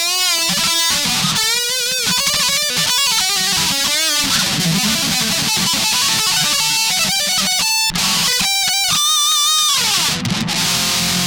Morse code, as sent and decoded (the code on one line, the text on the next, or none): none